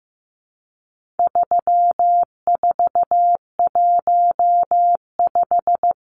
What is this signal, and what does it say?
Morse code, heard as a keyed tone, about 15 wpm, 700 Hz. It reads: 3415
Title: Numbers - 4-Digit 15wpm